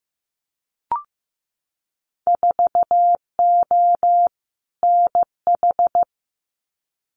Morse code, heard as a keyed tone, about 15 wpm, 700 Hz. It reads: 4O NH
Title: Winter Field Day Exchanges Encore 15wpm